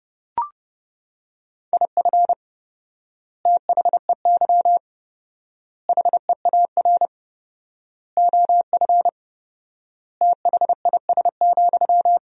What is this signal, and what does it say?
Morse code, heard as a keyed tone, about 31 wpm, 700 Hz. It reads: IF THEY HEAR OF THIS,
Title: 4x Word-Spacing - Sentences from Top 400 Words - 30wpm